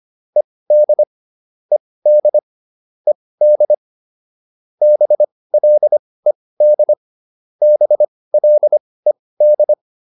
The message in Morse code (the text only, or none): ED ED ED BLED BLED